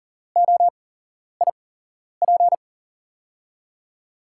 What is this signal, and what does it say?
Morse code, heard as a keyed tone, about 42 wpm, spaced as for 11 wpm, 700 Hz. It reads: OIP
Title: USA Call Signs - 8x Character Spacing - 40wpm